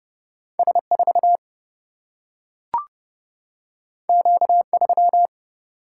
Morse code, heard as a keyed tone, about 30 wpm, 700 Hz. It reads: S4 Q3
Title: Single-Letter Single-Number 30wpm